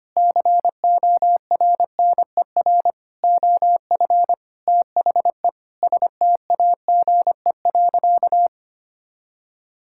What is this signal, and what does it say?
Morse code, heard as a keyed tone, about 25 wpm, 700 Hz. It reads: CORNER OF THE STAGE.
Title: Sentences from Top 1600 Words 25wpm